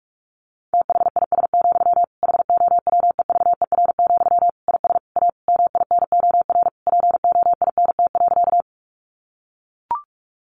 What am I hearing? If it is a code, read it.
Morse code, heard as a keyed tone, about 44 wpm, 700 Hz. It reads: THIS, HOWEVER, IS A MINOR POINT.
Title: Sentences from Top 1500 Words 45wpm